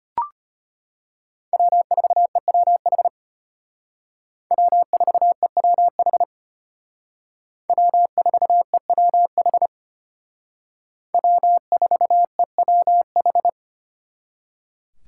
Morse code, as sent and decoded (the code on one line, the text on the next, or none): .-- ....- . .-- .... / .-- ....- . .-- .... / .-- ....- . .-- .... / .-- ....- . .-- ....
W4EWH W4EWH W4EWH W4EWH